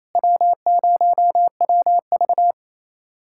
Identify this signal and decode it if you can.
Morse code, heard as a keyed tone, about 28 wpm, 700 Hz. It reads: W0WV
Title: USA Callsign DE Callsign - 4 Characters - 28wpm